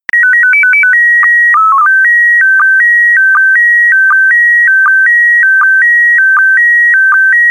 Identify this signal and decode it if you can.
SSTV (slow-scan television), PD 180.
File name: space comms pd180 sstv test recording1